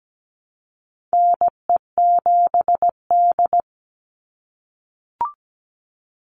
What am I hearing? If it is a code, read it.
Morse code, heard as a keyed tone, about 17 wpm, 700 Hz. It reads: NE7D
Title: Call Signs - 17wpm